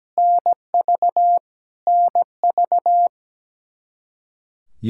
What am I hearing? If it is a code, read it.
Morse code, heard as a keyed tone, about 17 wpm, 700 Hz. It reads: NV NV
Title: K1USN SST QSO Practice 17wpm